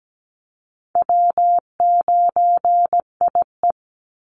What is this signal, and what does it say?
Morse code, heard as a keyed tone, about 17 wpm, 700 Hz. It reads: W9IE
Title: Easy Call Signs - 17wpm